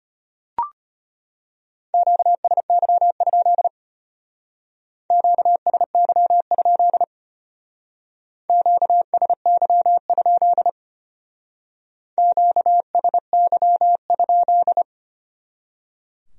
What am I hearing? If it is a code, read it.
Morse code, heard as a keyed tone, about 33 wpm, 700 Hz. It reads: QSY? QSY? QSY? QSY?